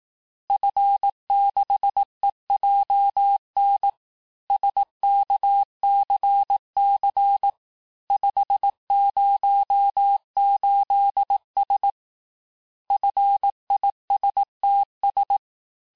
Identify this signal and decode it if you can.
Morse code, heard as a keyed tone, about 18 wpm, 800 Hz. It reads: F6EJN SKCC 508S FISTS
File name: cours-du-lundi files / cours-1852-FIN-du-QSO-vitesse-17MPM